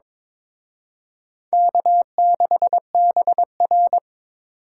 Morse code, heard as a keyed tone, about 26 wpm, 700 Hz. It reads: K6BR
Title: Speed Racing - USA 1x2 Call Signs 22wpm